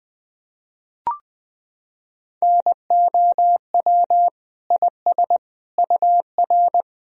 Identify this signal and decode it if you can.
Morse code, heard as a keyed tone, about 20 wpm, 700 Hz. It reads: NOW IS UR
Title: Sentences from Top 100 Words and Abbreviations 20wpm